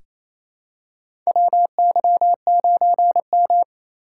Morse code, heard as a keyed tone, about 28 wpm, 700 Hz. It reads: WY9M
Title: USA Call Signs - 4 Characters - 2x Character Spacing - 28wpm